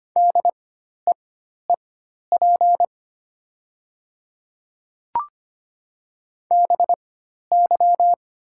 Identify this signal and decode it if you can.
Morse code, heard as a keyed tone, about 25 wpm, spaced as for 12 wpm, 700 Hz. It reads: DEEP BY
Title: 4x Character Spacing - Top 500 Words - 25wpm